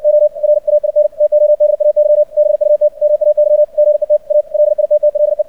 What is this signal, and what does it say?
Morse code, heard as a keyed tone, about 56 wpm, 600 Hz. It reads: OR AIN TODAY BUT MAY GET A LITTLE
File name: archived / CRSnet